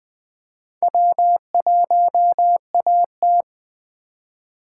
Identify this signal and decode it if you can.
Morse code, heard as a keyed tone, about 20 wpm, 700 Hz. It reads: W1AT
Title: Speed Racing - USA 1x2 Call Signs 17wpm